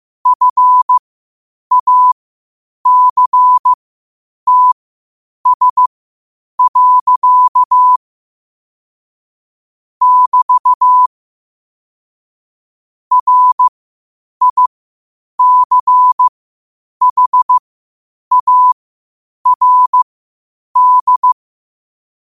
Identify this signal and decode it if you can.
Morse code, heard as a keyed tone, about 15 wpm, spaced as for 8 wpm, 1000 Hz. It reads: FACTS. <BT> RICHARD
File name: qotdfiles / 8wpm